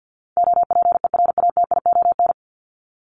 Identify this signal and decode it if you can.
Morse code, heard as a keyed tone, about 50 wpm, 700 Hz. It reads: OPERATION
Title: Top 1501-1750 Words 50wpm